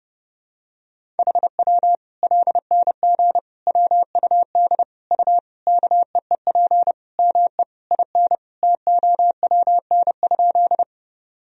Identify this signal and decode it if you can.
Morse code, heard as a keyed tone, about 32 wpm, 700 Hz. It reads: HW LNG WUD U KEEP ME IN TOWN?
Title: Sentences from Top 500 Words and Abbreviations 30wpm